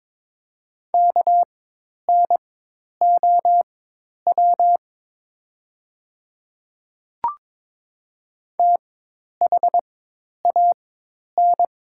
Morse code, heard as a keyed tone, about 22 wpm, spaced as for 10 wpm, 700 Hz. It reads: KNOW THAN
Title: 4x Character Spacing - Top 100 Words - 22wpm